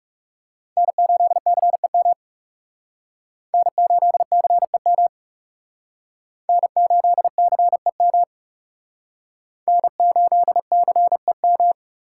Morse code, heard as a keyed tone, about 39 wpm, 700 Hz. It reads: N8CEM N8CEM N8CEM N8CEM